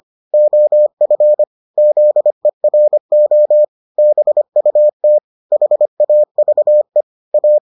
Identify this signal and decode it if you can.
Morse code, heard as a keyed tone, about 25 wpm, 600 Hz. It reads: OF ZERO BUT HAVE A